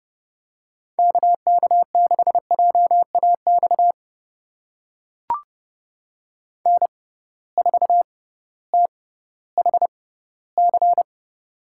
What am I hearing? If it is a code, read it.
Morse code, heard as a keyed tone, about 32 wpm, spaced as for 17 wpm, 700 Hz. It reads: KK6JAX N4THC
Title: USA Call Signs - 6x Character Spacing - 30wpm